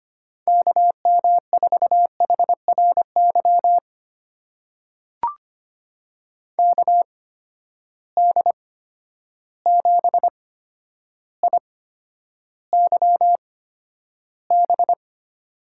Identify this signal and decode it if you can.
Morse code, heard as a keyed tone, about 25 wpm, spaced as for 11 wpm, 700 Hz. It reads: KM4HRY KD7IYB
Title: USA Call Signs - 6 Characters - 8x Character Spacing - 25wpm